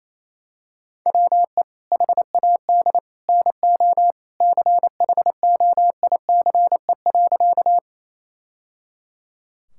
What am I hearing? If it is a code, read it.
Morse code, heard as a keyed tone, about 28 wpm, 700 Hz. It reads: WE HAD NO CHOICE.
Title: Sentences from Top 1000 Words - Review Part 6 - 28wpm